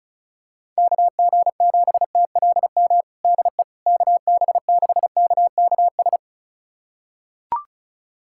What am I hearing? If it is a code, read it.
Morse code, heard as a keyed tone, about 34 wpm, 700 Hz. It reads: KG7TLM DE KB6KKS
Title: USA Callsign DE Callsign - 6 Characters - 35wpm